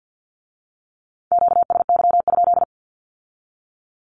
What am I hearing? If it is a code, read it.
Morse code, heard as a keyed tone, about 50 wpm, 700 Hz. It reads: QSY?
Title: Q-Codes with Spoken Meaning - 50wpm